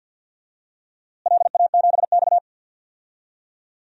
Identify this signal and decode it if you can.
Morse code, heard as a keyed tone, about 48 wpm, 700 Hz. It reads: LA7X
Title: DXCC-100 Call Signs - 4 Characters - Repeated 2x - 50wpm